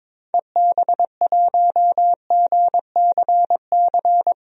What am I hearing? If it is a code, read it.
Morse code, heard as a keyed tone, about 22 wpm, 700 Hz. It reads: EB1GCC
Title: DXCC-100 Call Signs - 6 Characters - 22wpm